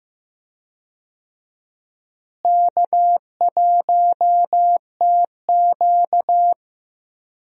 Morse code, heard as a keyed tone, about 15 wpm, 700 Hz. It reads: K1TQ
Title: USA Call Signs - 4 Characters - Repeated 2x - 15wpm